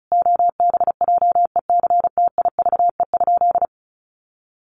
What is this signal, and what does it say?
Morse code, heard as a keyed tone, about 34 wpm, 700 Hz. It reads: OBJECTIVE?
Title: Sentences from Top 1200 Words 35wpm